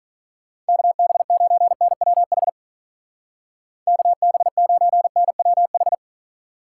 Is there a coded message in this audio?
Yes: Morse code from 0.7 to 5.9 s, heard as a keyed tone, about 45 wpm, 700 Hz. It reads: KB9NWH KB9NWH